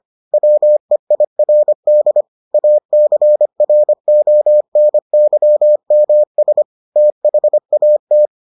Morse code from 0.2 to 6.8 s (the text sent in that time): WEIRD ACRONYMS